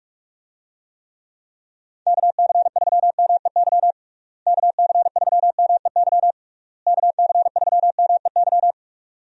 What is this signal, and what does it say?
Morse code, heard as a keyed tone, about 41 wpm, 700 Hz. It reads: KC3MEY KC3MEY KC3MEY
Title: USA Call Signs - 6 Characters - Repeated 3x - 45wpm